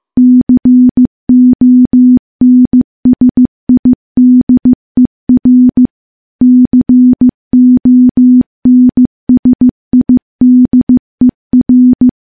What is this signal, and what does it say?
Morse code, heard as a keyed tone, about 16 wpm, 250 Hz. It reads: CONSIDER CONSIDER
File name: morse-training-15WPM